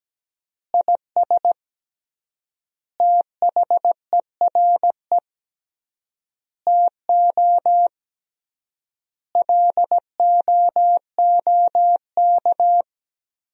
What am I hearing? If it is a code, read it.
Morse code, heard as a keyed tone, about 17 wpm, 700 Hz. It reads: IS THERE TO LOOK
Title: 3x Word-Spacing - Sentences from Top 100 Words - 17wpm